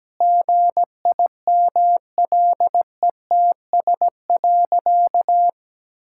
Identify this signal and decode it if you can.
Morse code, heard as a keyed tone, about 17 wpm, 700 Hz. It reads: GIMLETS.
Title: Pangram Challenge v2 - 17wpm